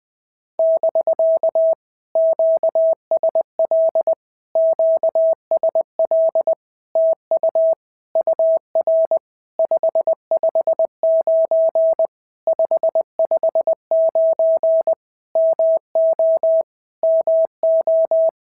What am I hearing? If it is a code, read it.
Morse code, heard as a keyed tone, about 20 wpm, 650 Hz. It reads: <BK> QSL QSL TU UR 559 559 MO MO
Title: POTA QSO Practice - 20wpm